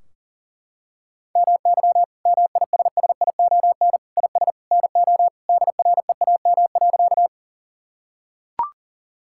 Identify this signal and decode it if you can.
Morse code, heard as a keyed tone, about 40 wpm, 700 Hz. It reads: MY MISSION IS NO DREAM.
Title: Sentences from Top 1600 Words 40wpm